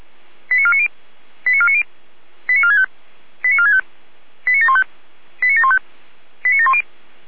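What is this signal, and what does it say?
Five-tone selective call (ZVEI), heard as quick runs of five tones, tones of about 70 ms. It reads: 89490 89490 89465 89465 89715 89715 89710
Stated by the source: ZVEI five-tone selective call received with an SDR.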